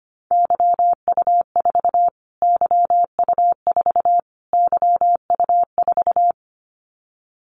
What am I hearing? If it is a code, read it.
Morse code, heard as a keyed tone, about 25 wpm, 700 Hz. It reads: YU4 YU4 YU4
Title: Callsign Prefixes for Top 25 DXCC - Repeated 3x - 25wpm